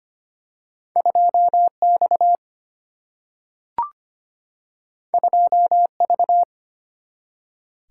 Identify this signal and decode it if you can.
Morse code, heard as a keyed tone, about 25 wpm, 700 Hz. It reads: 2X 2V
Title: Single-Letter Single-Number 25wpm